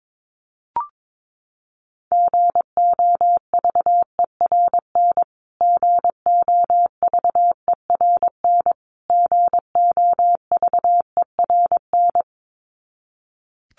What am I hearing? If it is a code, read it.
Morse code, heard as a keyed tone, about 22 wpm, 700 Hz. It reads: GOVERN GOVERN GOVERN
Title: Top 500 Words Repeated 22wpm